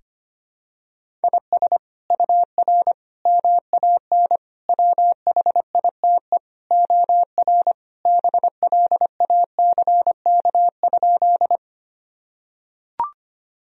Morse code, heard as a keyed tone, about 25 wpm, 700 Hz. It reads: IS UR MAN WHITE OR BLACK?
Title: Sentences from Top 400 Words and Abbreviations 25wpm